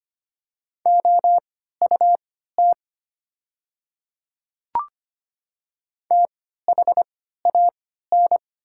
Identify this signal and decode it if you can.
Morse code, heard as a keyed tone, about 25 wpm, spaced as for 14 wpm, 690 Hz. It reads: OUT THAN
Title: 3x Character Spacing - Top 100 Words - 25wpm